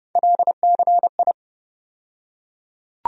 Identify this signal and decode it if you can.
Morse code, heard as a keyed tone, about 30 wpm, 700 Hz. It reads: LCI
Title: USA Call Sign Postfixes - 3 Letters 30wpm